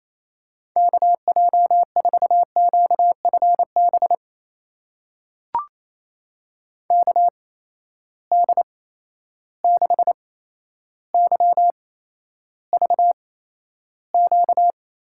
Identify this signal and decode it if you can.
Morse code, heard as a keyed tone, about 28 wpm, spaced as for 12 wpm, 700 Hz. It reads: KJ4QFB KD6YVQ
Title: USA Call Signs - 6 Characters - 8x Character Spacing - 28wpm